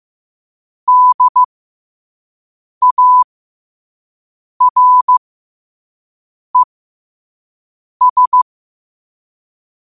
Morse code, heard as a keyed tone, about 15 wpm, spaced as for 5 wpm, 1000 Hz. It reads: DARES
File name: qotdfiles / 5wpm